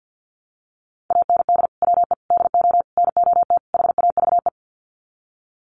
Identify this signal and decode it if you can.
Morse code, heard as a keyed tone, about 50 wpm, 700 Hz. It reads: AND WE DO NOT HAVE
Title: Sets of 5 Words Encore 50wpm